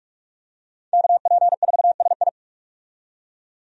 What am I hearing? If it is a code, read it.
Morse code, heard as a keyed tone, about 43 wpm, 700 Hz. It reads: KP4SI
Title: DXCC-100 Call Signs - 5 Characters - 45wpm